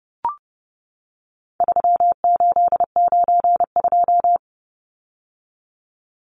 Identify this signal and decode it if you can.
Morse code, heard as a keyed tone, about 30 wpm, 700 Hz. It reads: 3892
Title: Numbers - 4-Digit 30wpm